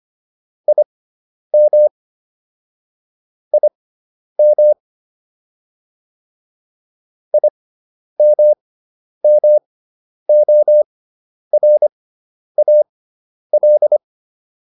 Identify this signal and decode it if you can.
Morse code, heard as a keyed tone, about 25 wpm, spaced as for 10 wpm, 600 Hz. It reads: IM IM IMMORAL